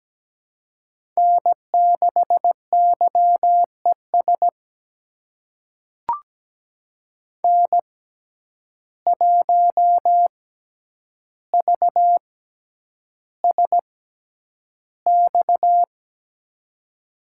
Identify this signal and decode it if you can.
Morse code, heard as a keyed tone, about 17 wpm, spaced as for 9 wpm, 700 Hz. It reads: N6YES N1VSX
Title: USA Call Signs - 5 Characters - 6x Character Spacing - 17wpm